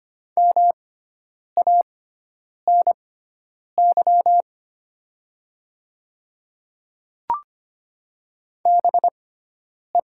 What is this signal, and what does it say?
Morse code, heard as a keyed tone, about 25 wpm, spaced as for 9 wpm, 700 Hz. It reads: MANY BE
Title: 6x Character Spacing - Top 100 Words - 25wpm